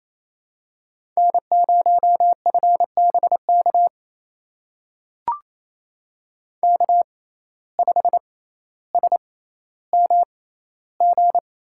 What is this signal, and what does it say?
Morse code, heard as a keyed tone, about 28 wpm, spaced as for 14 wpm, 700 Hz. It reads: N0FBK K5SMG
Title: USA Call Signs - 5 Characters - 6x Character Spacing - 28wpm